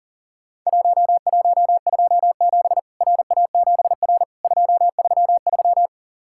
Morse code, heard as a keyed tone, about 39 wpm, 700 Hz. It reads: 1127 RA7R 233
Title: Worked All Europe DX Contest - QTC Exchange - 40wpm